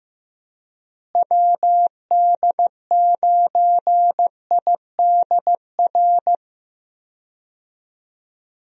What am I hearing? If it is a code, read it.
Morse code, heard as a keyed tone, about 15 wpm, 700 Hz. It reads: WD9IDR
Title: USA Call Signs - 6 Characters - Repeated 2x - 15wpm